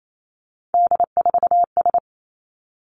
Morse code, heard as a keyed tone, about 28 wpm, 700 Hz. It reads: D4S
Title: Three Challenging Letter-Number Combinations 28wpm